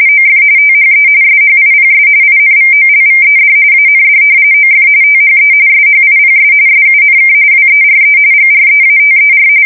RTTY, about 45.45 baud. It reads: OF THE SHIPS INVOLVED AND OF THE SHIPS SUNK WERE GREATER
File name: fsk170 / battleshipnj